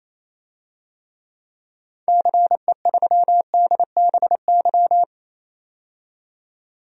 Morse code, heard as a keyed tone, about 28 wpm, 700 Hz. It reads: CE3DBY
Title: DXCC-100 Call Signs - 6 Characters - 28wpm